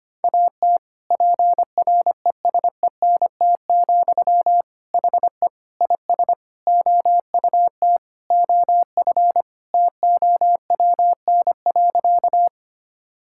Morse code, heard as a keyed tone, about 25 wpm, 700 Hz. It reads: AT PRESENT, HE IS OUT OF TOWN.
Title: Sentences from Top 1000 Words - Review Part 5 - 25wpm